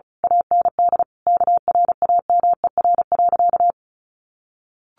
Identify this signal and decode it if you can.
Morse code, heard as a keyed tone, about 34 wpm, 700 Hz. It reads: AND KRAMER.
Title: A Sound Of Thunder 35wpm